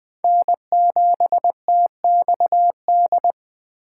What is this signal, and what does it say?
Morse code, heard as a keyed tone, about 23 wpm, 700 Hz. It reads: N7TXD
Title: Speed Racing - USA 1x3 Call Signs 20wpm